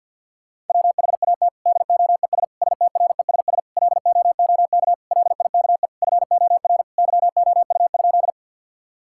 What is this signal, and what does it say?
Morse code, heard as a keyed tone, about 50 wpm, 700 Hz. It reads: WHAT DOES STRESS LOOK LIKE FOR YOU?
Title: Sentences from Top 1000 Words - Review Part 6 - 50wpm